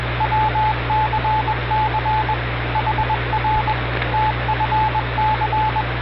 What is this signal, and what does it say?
Morse code, heard as a keyed tone, about 21 wpm, 870 Hz, with radio noise. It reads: WCC HR TFC